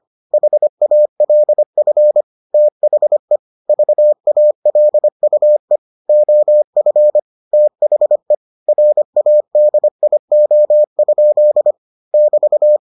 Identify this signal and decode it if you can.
Morse code, heard as a keyed tone, about 25 wpm, 600 Hz. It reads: HALF THE VALUE OF THE RADIO? <BT>